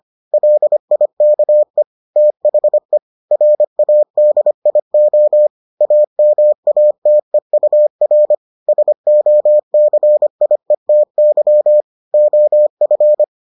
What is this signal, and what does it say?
Morse code, heard as a keyed tone, about 25 wpm, 600 Hz. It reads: LIKE THE RADIO AMATEUR SOCIETY OF